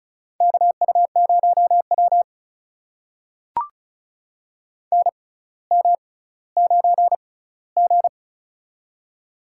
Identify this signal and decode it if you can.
Morse code, heard as a keyed tone, about 35 wpm, spaced as for 18 wpm, 700 Hz. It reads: KU0W NM9G
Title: USA Call Signs - 4 Characters - 6x Character Spacing - 35wpm